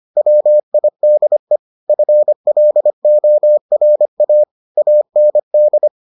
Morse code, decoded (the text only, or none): WIDE FLORA AND